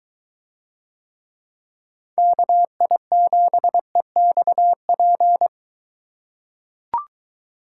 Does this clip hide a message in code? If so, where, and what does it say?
Yes: Morse code from 2.2 to 5.5 s, heard as a keyed tone, about 23 wpm, 700 Hz. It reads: KI7EXP